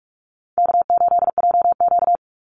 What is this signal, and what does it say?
Morse code, heard as a keyed tone, about 44 wpm, 700 Hz. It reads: K8JQ
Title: Easy Call Signs Repeated 2x - 45wpm